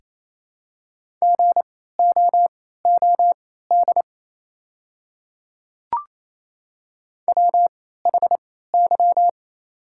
Morse code, heard as a keyed tone, about 28 wpm, spaced as for 16 wpm, 700 Hz. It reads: GOOD WHY
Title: 3x Character Spacing - Top 200 Words - 28wpm